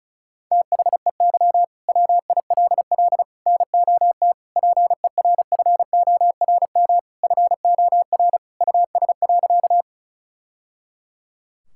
Morse code, heard as a keyed tone, about 36 wpm, 700 Hz. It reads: THEY WILL NOT PERFORM FOR US.
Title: Sentences from Top 1000 Words - Review Part 5 - 35wpm